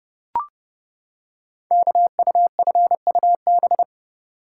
Morse code, heard as a keyed tone, about 30 wpm, 700 Hz. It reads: KUFUB
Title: Code Groups - 5 Letters - 30wpm